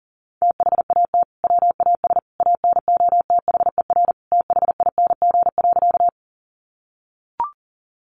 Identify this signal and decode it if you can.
Morse code, heard as a keyed tone, about 40 wpm, 700 Hz. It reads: THAT WAS ANOTHER THING.
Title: Sentences from Top 600 Words 40wpm